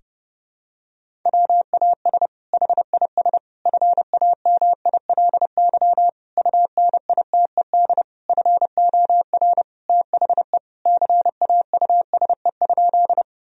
Morse code, heard as a keyed tone, about 30 wpm, 700 Hz. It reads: WAS HIS FAMILY UNITED FOR THE CAUSE?
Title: Sentences from Top 1000 Words - Review Part 4 - 30wpm